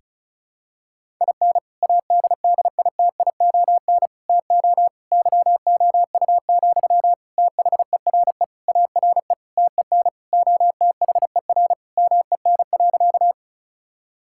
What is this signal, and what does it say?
Morse code, heard as a keyed tone, about 36 wpm, 700 Hz. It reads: IN ADDITION TO YOU, THERE ARE TEN OTHER MEN.
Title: Sentences from Top 1000 Words - Review Part 6 - 35wpm